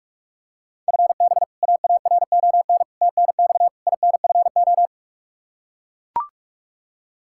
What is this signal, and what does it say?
Morse code, heard as a keyed tone, about 42 wpm, 700 Hz. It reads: FB AARON TNX INFO
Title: Practice QSO - Exchange 3 - More Info - 45wpm